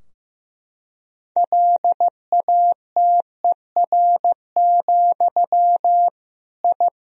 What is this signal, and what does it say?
Morse code, heard as a keyed tone, about 15 wpm, 700 Hz. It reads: LATER, I
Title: Sentences from Top 1000 Words - Review Part 4 - 15wpm